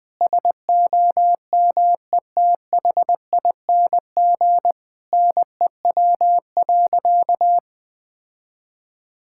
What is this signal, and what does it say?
Morse code, heard as a keyed tone, about 20 wpm, 700 Hz. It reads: SOMETHING NEW.
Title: Sentences with 8-Word Phrases - 20wpm